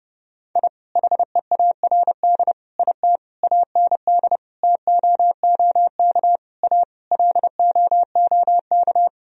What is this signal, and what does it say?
Morse code, heard as a keyed tone, about 30 wpm, 700 Hz. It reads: I HEARD IT AND TOOK A LOOK
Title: Sentences from Top 1000 Words - Review Part 1 - 30wpm